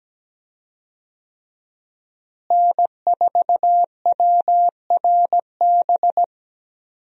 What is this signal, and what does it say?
Morse code, heard as a keyed tone, about 17 wpm, 700 Hz. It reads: N4WRB
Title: USA Call Signs - 5 Characters - Repeated 3x - 17wpm